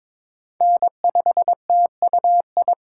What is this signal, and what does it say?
Morse code, heard as a keyed tone, about 22 wpm, 700 Hz. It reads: N5TUI
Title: USA Call Signs - 5 Characters - 4x Character Spacing - 22wpm